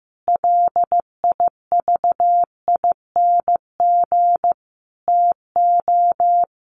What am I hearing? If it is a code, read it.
Morse code, heard as a keyed tone, about 15 wpm, 700 Hz. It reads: LIVING TO
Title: Sentences from Top 1500 Words 15wpm